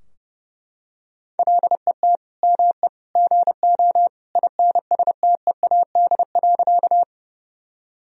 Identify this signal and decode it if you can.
Morse code, heard as a keyed tone, about 30 wpm, 700 Hz. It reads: LET ME GO INSTEAD.
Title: Sentences from Top 1000 Words - Review Part 4 - 30wpm